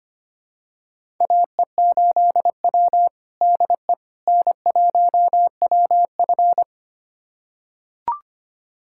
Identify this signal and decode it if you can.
Morse code, heard as a keyed tone, about 25 wpm, 700 Hz. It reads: AE8W DE N1WF
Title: USA Callsign DE Callsign - 4 Characters - 25wpm